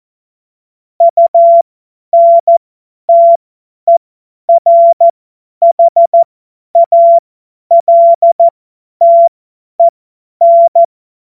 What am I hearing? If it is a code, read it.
Morse code, heard as a keyed tone, about 14 wpm, spaced as for 10 wpm, 680 Hz. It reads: UNTERHALTEN
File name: Morsekurs / Witze / Witz 26 März 2024 50 F70